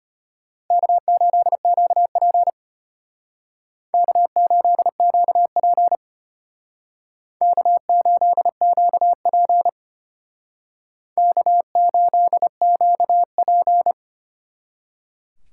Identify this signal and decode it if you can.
Morse code, heard as a keyed tone, about 33 wpm, 700 Hz. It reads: K8QP K8QP K8QP K8QP